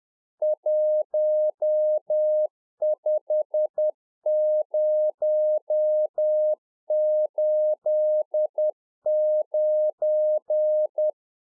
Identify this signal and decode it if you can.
Morse code, heard as a keyed tone, about 10 wpm, 610 Hz. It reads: 15089
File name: 2011-04-mp3 / NEW MEMBERS 10WPM